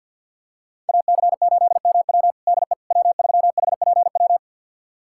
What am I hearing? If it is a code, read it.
Morse code, heard as a keyed tone, about 51 wpm, 700 Hz. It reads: AC8MW DE W3HPW
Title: USA Callsign DE Callsign - 5 Characters - 50wpm